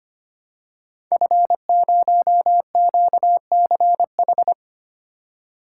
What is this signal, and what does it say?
Morse code, heard as a keyed tone, about 25 wpm, 700 Hz. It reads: F0QCH
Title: Code Groups - 5 Letter:Numbers - 25wpm